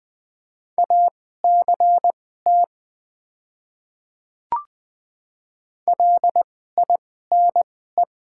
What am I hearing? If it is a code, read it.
Morse code, heard as a keyed tone, about 20 wpm, spaced as for 14 wpm, 700 Hz. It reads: ACT LINE
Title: 2x Character Spacing - Top 300 Words - 20wpm